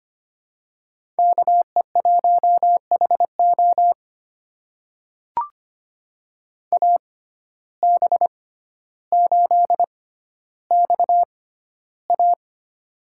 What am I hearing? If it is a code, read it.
Morse code, heard as a keyed tone, about 25 wpm, spaced as for 13 wpm, 700 Hz. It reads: KE1HO AB8XA
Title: USA Call Signs - 5 Characters - 6x Character Spacing - 25wpm